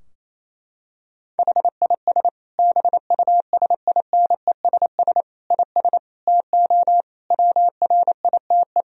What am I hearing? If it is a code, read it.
Morse code, heard as a keyed tone, about 28 wpm, 700 Hz. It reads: HIS BUSINESS IS TO WRITE
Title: Sentences from Top 1000 Words - Review Part 3 - 28wpm